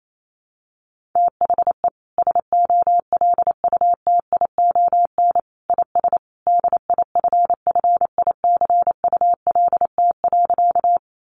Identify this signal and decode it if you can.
Morse code, heard as a keyed tone, about 28 wpm, 700 Hz. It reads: THE SOLUTION IS DIFFICULT.
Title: Sentences from Top 1300 Words 28wpm